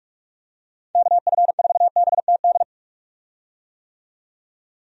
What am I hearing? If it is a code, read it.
Morse code, heard as a keyed tone, about 46 wpm, 700 Hz. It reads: KF4BTD
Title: USA Call Signs - 6 Characters - 45wpm